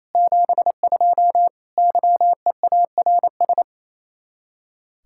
Morse code, heard as a keyed tone, about 37 wpm, 700 Hz. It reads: 72 YEARS